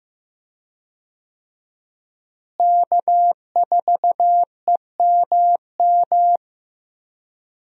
Morse code, heard as a keyed tone, about 15 wpm, 700 Hz. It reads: K4EMM
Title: USA Call Signs - 5 Characters - Repeated 2x - 15wpm